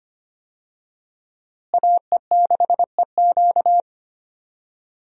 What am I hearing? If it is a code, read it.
Morse code, heard as a keyed tone, about 25 wpm, 700 Hz. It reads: AE6EQ
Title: USA Call Signs - 5 Characters - Repeated 3x - 25wpm